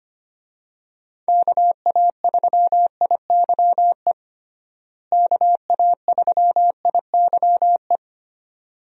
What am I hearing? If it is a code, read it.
Morse code, heard as a keyed tone, about 25 wpm, 700 Hz. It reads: KA3IYE KA3IYE
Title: USA Call Signs - 6 Characters - Repeated 2x - 25wpm